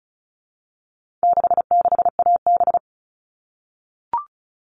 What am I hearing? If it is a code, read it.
Morse code, heard as a keyed tone, about 34 wpm, 700 Hz. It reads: 66AB
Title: All Cut Numbers - 4-Digit 35wpm